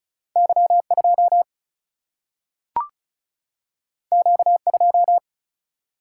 Morse code, heard as a keyed tone, about 36 wpm, 700 Hz. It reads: Y2 Q2
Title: Single-Letter Single-Number 35wpm